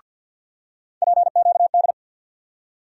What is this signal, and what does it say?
Morse code, heard as a keyed tone, about 47 wpm, 700 Hz. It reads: PQD
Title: USA Call Sign Postfixes - 3 Letters 50wpm